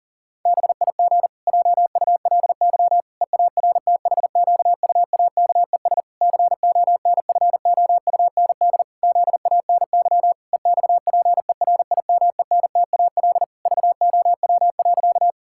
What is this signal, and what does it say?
Morse code, heard as a keyed tone, about 40 wpm, 700 Hz. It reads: BIG JULY EARTHQUAKES CONFOUND ZANY EXPERIMENTAL VOW.
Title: Pangram Challenge v2 - 40wpm